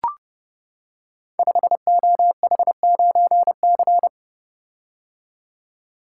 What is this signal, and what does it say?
Morse code, heard as a keyed tone, about 28 wpm, 700 Hz. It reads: 5OH9C
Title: Code Groups - 5 Letter:Numbers - 30wpm